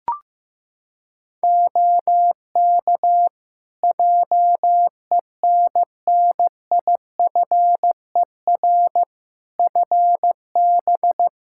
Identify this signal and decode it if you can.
Morse code, heard as a keyed tone, about 16 wpm, 700 Hz. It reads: OK JENNIFER FB
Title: Practice QSO - Exchange 2 - Info - 15wpm